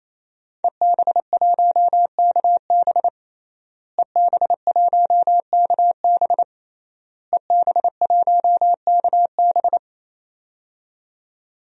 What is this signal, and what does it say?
Morse code, heard as a keyed tone, about 28 wpm, 700 Hz. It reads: EB1KB EB1KB EB1KB
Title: DXCC-100 Call Signs - 5 Characters - Repeated 3x - 28wpm